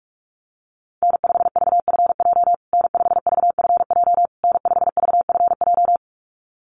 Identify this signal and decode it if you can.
Morse code, heard as a keyed tone, about 44 wpm, 700 Hz. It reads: N5VFJ N5VFJ N5VFJ
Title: USA Call Signs Repeated 3x 45wpm